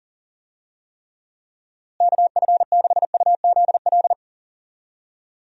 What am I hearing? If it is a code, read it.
Morse code, heard as a keyed tone, about 40 wpm, 700 Hz. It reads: KF6UZL
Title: USA Call Signs - 6 Characters - Repeated 2x - 40wpm